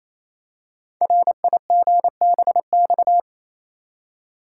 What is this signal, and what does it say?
Morse code, heard as a keyed tone, about 27 wpm, 700 Hz. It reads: RIGBX
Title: Code Groups - 5 Letters - 28wpm